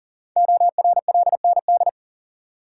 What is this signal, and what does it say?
Morse code, heard as a keyed tone, about 39 wpm, 700 Hz. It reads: ORLND
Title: Code Groups - 5 Letters - 40wpm